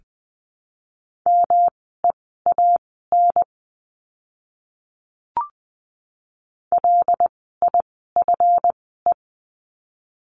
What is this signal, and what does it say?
Morse code, heard as a keyed tone, about 20 wpm, spaced as for 15 wpm, 700 Hz. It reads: MEAN LIFE
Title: 230 Lesson 37f - Letter x - Words 20wpm